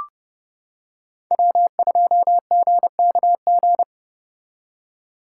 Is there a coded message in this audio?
Yes: Morse code from 1.3 to 3.8 s, heard as a keyed tone, about 29 wpm, 700 Hz. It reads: W2GKG